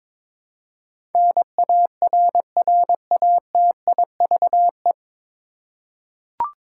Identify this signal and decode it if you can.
Morse code, heard as a keyed tone, about 21 wpm, 700 Hz. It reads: NARRATIVE
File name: 9 Letter Words - First 3 Letters Given - 22wpm